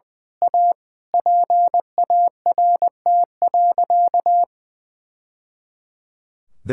Morse code, heard as a keyed tone, about 21 wpm, 700 Hz. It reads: A PART.
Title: Sentences from Top 1000 Words - Review Part 5 - 20wpm